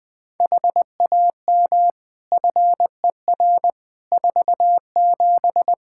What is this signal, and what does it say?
Morse code, heard as a keyed tone, about 20 wpm, 700 Hz. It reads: HAM FER 47
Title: K1USN SST QSO Practice 20wpm